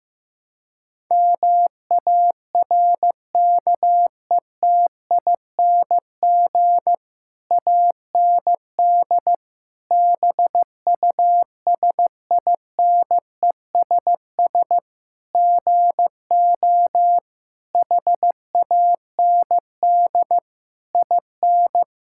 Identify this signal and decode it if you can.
Morse code, heard as a keyed tone, about 15 wpm, 700 Hz. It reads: MARKETING AND BUSINESS GO HAND IN
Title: Sentences from Top 1000 Words - Review Part 5 - 15wpm